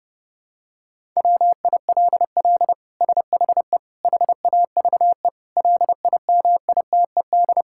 Morse code, heard as a keyed tone, about 28 wpm, 700 Hz. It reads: WILL SHE HAVE LIMITED
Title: Sentences from Top 1000 Words - Review Part 6 - 30wpm